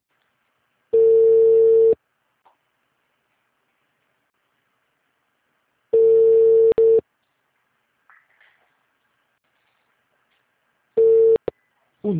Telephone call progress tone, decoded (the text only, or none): ringback tone